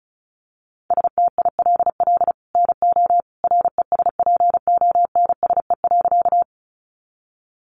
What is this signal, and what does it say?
Morse code, heard as a keyed tone, about 34 wpm, 700 Hz. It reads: STILL NO RESPONSE.
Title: Sentences from Top 1400 Words 35wpm